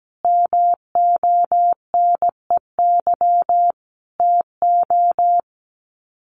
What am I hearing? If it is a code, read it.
Morse code, heard as a keyed tone, about 17 wpm, 700 Hz. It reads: MONEY TO
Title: Sets of 5 Words Encore2 17wpm